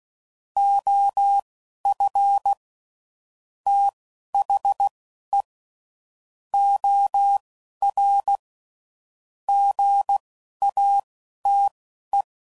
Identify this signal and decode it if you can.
Morse code, heard as a keyed tone, about 16 wpm, spaced as for 11 wpm, 800 Hz. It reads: OF THE OR GATE
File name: CW Code Groups @10WPM / Technical Subjects / OR Gate